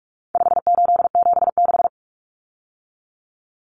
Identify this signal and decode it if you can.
Morse code, heard as a keyed tone, about 44 wpm, 700 Hz. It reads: 5876
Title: Numbers - 4-Digit 45wpm